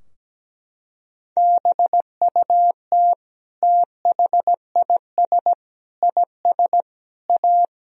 Morse code, heard as a keyed tone, about 17 wpm, 700 Hz. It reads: BUT THIS IS A
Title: Sentences from Top 1000 Words - Review Part 3 - 17wpm